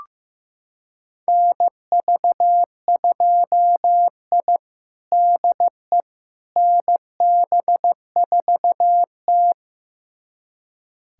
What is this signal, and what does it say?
Morse code, heard as a keyed tone, about 15 wpm, 700 Hz. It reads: NV2I DE NB4T
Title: USA Callsign DE Callsign - 4 Characters - 15wpm